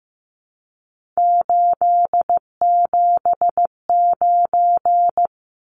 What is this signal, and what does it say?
Morse code, heard as a keyed tone, about 15 wpm, 700 Hz. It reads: 879
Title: Numbers - 3-Digit 15wpm